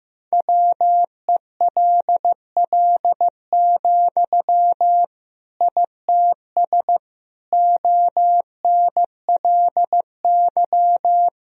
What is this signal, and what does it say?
Morse code, heard as a keyed tone, about 15 wpm, 700 Hz. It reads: WELL, ITS ONLY
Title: Sentences from Top 1000 Words - Review Part 3 - 15wpm